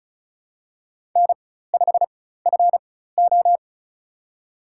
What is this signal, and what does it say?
Morse code, heard as a keyed tone, about 36 wpm, spaced as for 16 wpm, 700 Hz. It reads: N5FO
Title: USA Call Signs - 4x Character Spacing - 35wpm